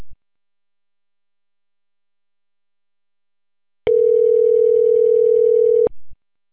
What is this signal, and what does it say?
Ringback tone (North America), 440 + 480 Hz.